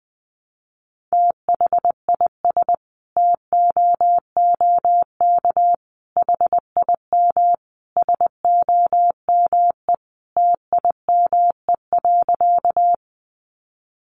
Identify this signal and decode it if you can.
Morse code, heard as a keyed tone, about 20 wpm, 700 Hz. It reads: THIS TOOK HIM SOME TIME.
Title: Sentences from Top 300 Words 20wpm